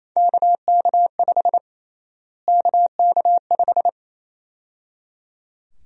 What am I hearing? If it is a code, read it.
Morse code, heard as a keyed tone, about 28 wpm, 700 Hz. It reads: KK5 KK5
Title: USA Call Sign Prefixes - Repeated 2x - 28wpm